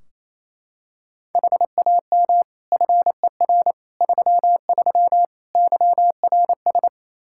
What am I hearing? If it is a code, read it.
Morse code, heard as a keyed tone, about 27 wpm, 700 Hz. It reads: HAM FER 33 YRS
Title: Practice QSO - Exchange 3 - More Info - 28wpm